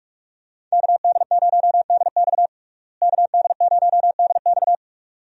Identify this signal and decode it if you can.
Morse code, heard as a keyed tone, about 42 wpm, 700 Hz. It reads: KD0DX KD0DX
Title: USA Call Signs - 5 Characters - Repeated 2x - 45wpm